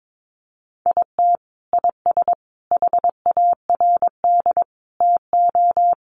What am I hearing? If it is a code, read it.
Morse code, heard as a keyed tone, about 22 wpm, 700 Hz. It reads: IT IS HARD TO
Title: Sets of 5 Words Encore2 22wpm